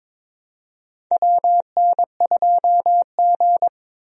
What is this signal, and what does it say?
Morse code, heard as a keyed tone, about 22 wpm, 700 Hz. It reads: WN2G
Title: USA Call Signs - 4 Characters - Repeated 2x - 22wpm